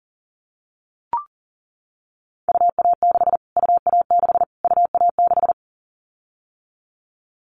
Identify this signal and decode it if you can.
Morse code, heard as a keyed tone, about 40 wpm, 700 Hz. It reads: UA6 UA6 UA6
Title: Callsign Prefixes for Top 25 DXCC - Repeated 3x - 40wpm